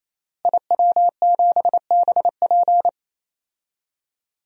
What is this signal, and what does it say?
Morse code, heard as a keyed tone, about 28 wpm, 700 Hz. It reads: IW7BP
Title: DXCC-100 Call Signs - 5 Characters - Repeated 3x - 28wpm